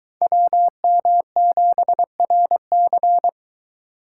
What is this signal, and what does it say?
Morse code, heard as a keyed tone, about 21 wpm, 700 Hz. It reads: WM7RC